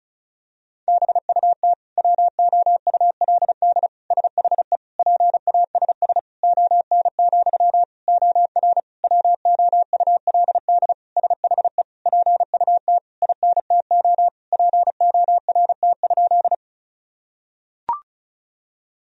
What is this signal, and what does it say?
Morse code, heard as a keyed tone, about 36 wpm, 700 Hz. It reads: BUT WOULD SHE PASS ON, OR WOULD SHE PUT INTO PORT?
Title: Sentences from Top 1000 Words - Review Part 1 - 35wpm